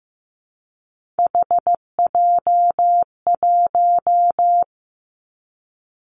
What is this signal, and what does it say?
Morse code, heard as a keyed tone, about 15 wpm, 700 Hz. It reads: HJ1
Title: Three Challenging Letter-Number Combinations 15wpm